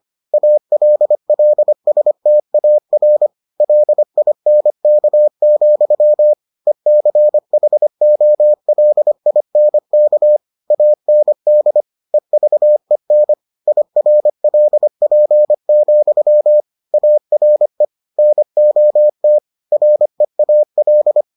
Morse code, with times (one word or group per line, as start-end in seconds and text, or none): 0.3-3.3 ALLSTAR
3.6-6.3 LINK,
6.7-10.4 ECHOLINK
10.7-11.8 AND
12.1-13.3 EVEN
13.7-16.6 IRLP,
16.9-17.9 ARE
18.2-19.4 NOT
19.7-21.2 REAL